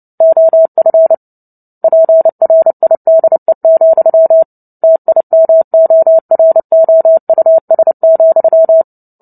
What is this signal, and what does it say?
Morse code, heard as a keyed tone, about 30 wpm, 650 Hz. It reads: OF PRIDE, TIMOROUS,
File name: Book Two / 03-Chapter 3 - 29 WPM